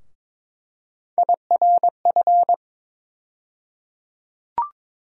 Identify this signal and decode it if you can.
Morse code, heard as a keyed tone, about 21 wpm, 700 Hz. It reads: IRF
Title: USA Call Sign Postfixes - 3 Letters 22wpm